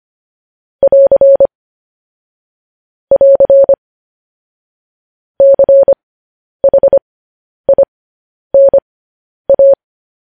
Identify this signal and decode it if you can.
Morse code, heard as a keyed tone, about 25 wpm, spaced as for 10 wpm, 550 Hz. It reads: <AR> <AR> CHINA